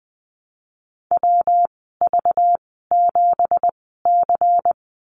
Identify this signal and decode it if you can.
Morse code, heard as a keyed tone, about 20 wpm, spaced as for 15 wpm, 700 Hz. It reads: WV7C
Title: 200 Lesson 32g - Letter k - Call Signs 20wpm